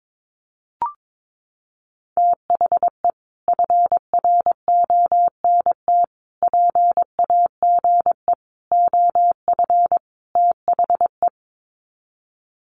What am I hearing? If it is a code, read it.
Morse code, heard as a keyed tone, about 22 wpm, 700 Hz. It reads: THE FRONT PAGE OF THE
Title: Sets of 5 Words Encore 22wpm